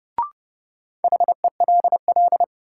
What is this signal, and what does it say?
Morse code, heard as a keyed tone, about 32 wpm, 700 Hz. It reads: HELL
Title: Sets of 5 Words Encore - New Word Drill - 30wpm